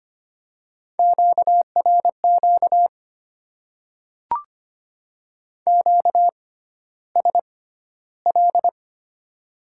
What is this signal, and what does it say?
Morse code, heard as a keyed tone, about 25 wpm, spaced as for 13 wpm, 700 Hz. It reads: QRQ QSL
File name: Top 90 QSO Elements - 6x Character Spacing - 25wpm